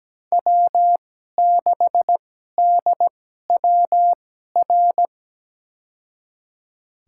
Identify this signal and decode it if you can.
Morse code, heard as a keyed tone, about 17 wpm, spaced as for 12 wpm, 700 Hz. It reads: W6DWR
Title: USA Call Signs - 5 Characters - 2x Character Spacing - 17wpm